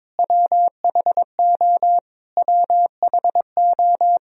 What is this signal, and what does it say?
Morse code, heard as a keyed tone, about 22 wpm, 700 Hz. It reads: WHO WHO
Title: Top 100 Words Repeated 22wpm